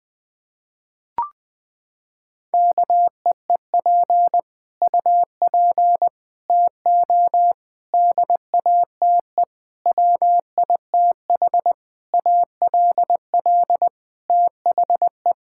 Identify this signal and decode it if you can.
Morse code, heard as a keyed tone, about 20 wpm, 700 Hz. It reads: KEEP UP TO DATE WITH ALL THE
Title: Sets of 9 Words - 20wpm